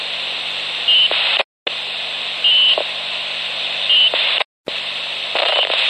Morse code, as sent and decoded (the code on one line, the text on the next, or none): none